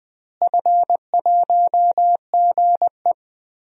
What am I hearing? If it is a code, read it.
Morse code, heard as a keyed tone, about 20 wpm, 700 Hz. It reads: F1GE
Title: DXCC-100 Call Signs - 4 Characters - Repeated 2x - 20wpm